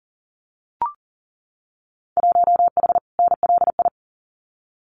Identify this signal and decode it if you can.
Morse code, heard as a keyed tone, about 38 wpm, 700 Hz. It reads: 1H NLI
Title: Winter Field Day Exchanges 40wpm